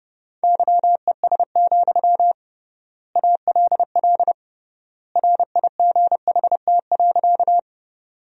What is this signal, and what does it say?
Morse code, heard as a keyed tone, about 30 wpm, 700 Hz. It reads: YES, ALL RIGHT.
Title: 3x Word-Spacing - Sentences from Top 500 Words - 30wpm